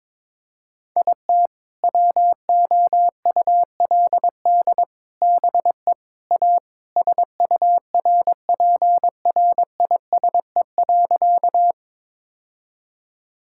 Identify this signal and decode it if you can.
Morse code, heard as a keyed tone, about 22 wpm, 700 Hz. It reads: IT WOULD BE A SURPRISE.
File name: practice / 22wpm / Sentences with 4-Word Phrases - 22wpm